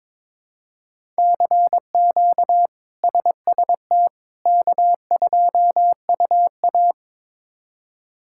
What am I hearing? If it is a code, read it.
Morse code, heard as a keyed tone, about 22 wpm, 700 Hz. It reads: CQ SST K2UA
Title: K1USN SST QSO Practice - 22wpm